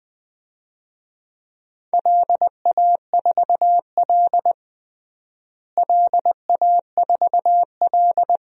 Morse code, heard as a keyed tone, about 21 wpm, 700 Hz. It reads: LA4L LA4L
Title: DXCC-100 Call Signs - 4 Characters - Repeated 3x - 20wpm